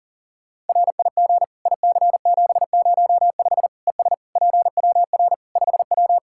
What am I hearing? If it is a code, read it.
Morse code, heard as a keyed tone, about 40 wpm, 700 Hz. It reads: RIG IC705 ES PWR 5W
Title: Practice QSO - Exchange 2 - Info - 40wpm